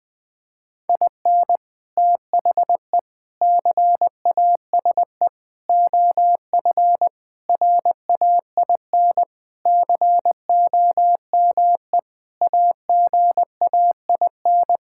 Morse code, heard as a keyed tone, about 20 wpm, 700 Hz. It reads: IN THE CASE OF RAIN COME AGAIN
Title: Sentences with 4-Word Phrases - Spoken Beginning - 20wpm